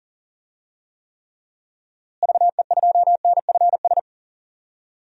Morse code, heard as a keyed tone, about 40 wpm, 700 Hz. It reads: VE2NFS
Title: DXCC-100 Call Signs - 6 Characters - Repeated 2x - 40wpm